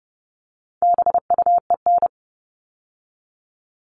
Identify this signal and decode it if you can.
Morse code, heard as a keyed tone, about 30 wpm, 700 Hz. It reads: BUEN
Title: All Cut Numbers - 4-Digit 30wpm